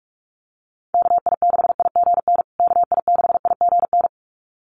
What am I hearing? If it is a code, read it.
Morse code, heard as a keyed tone, about 44 wpm, 700 Hz. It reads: KI6IGN KI6IGN
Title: USA Call Signs Repeated 2x 45wpm